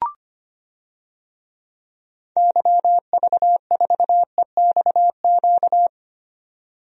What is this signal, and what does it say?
Morse code, heard as a keyed tone, about 25 wpm, 700 Hz. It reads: YV4EXQ
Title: DXCC-100 Call Signs - 6 Characters - Repeated 2x - 25wpm